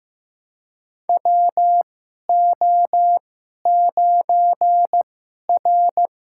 Morse code, heard as a keyed tone, about 15 wpm, spaced as for 11 wpm, 700 Hz. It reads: WO9R
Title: USA Call Signs - 4 Characters - 2x Character Spacing - 15wpm